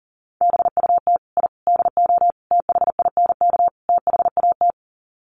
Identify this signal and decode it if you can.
Morse code, heard as a keyed tone, about 40 wpm, 700 Hz. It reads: BUT I DO THINK THAT
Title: Sets of 5 Words Encore2 40wpm